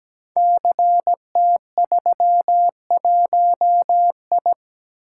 Morse code, heard as a keyed tone, about 17 wpm, 690 Hz. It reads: CT31I
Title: Code Groups - 5 Letter:Numbers - 17wpm